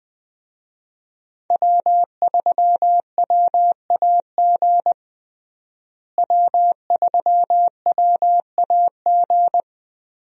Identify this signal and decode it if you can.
Morse code, heard as a keyed tone, about 20 wpm, 700 Hz. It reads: W3WAG W3WAG
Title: USA Call Signs - 5 Characters - Repeated 2x - 20wpm